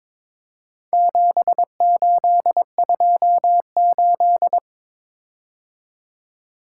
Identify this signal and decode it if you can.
Morse code, heard as a keyed tone, about 22 wpm, 700 Hz. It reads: 7828
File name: practice / 22wpm / Four Letter-Number Combinations - 22wpm